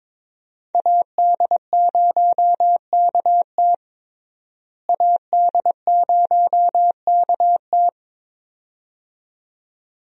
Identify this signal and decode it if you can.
Morse code, heard as a keyed tone, about 22 wpm, 700 Hz. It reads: AD0KT AD0KT
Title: USA Call Signs - 5 Characters - Repeated 2x - 22wpm